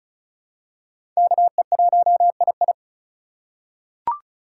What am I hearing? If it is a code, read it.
Morse code, heard as a keyed tone, about 34 wpm, 700 Hz. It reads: KE1II
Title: USA Call Signs - 6x Character Spacing - 35wpm